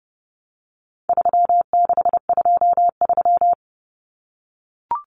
Morse code, heard as a keyed tone, about 30 wpm, 700 Hz. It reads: 3623
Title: Numbers - 4-Digit 30wpm